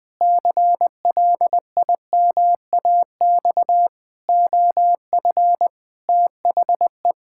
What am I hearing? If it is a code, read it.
Morse code, heard as a keyed tone, about 20 wpm, 700 Hz. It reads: CLIMAX OF THE
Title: Pangram Challenge v2 - 20wpm